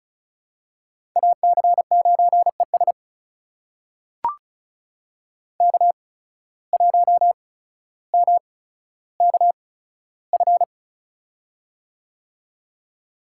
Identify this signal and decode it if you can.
Morse code, heard as a keyed tone, about 36 wpm, spaced as for 15 wpm, 700 Hz. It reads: AC9ES K1MKF
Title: USA Call Signs - 5 Characters - 8x Character Spacing - 35wpm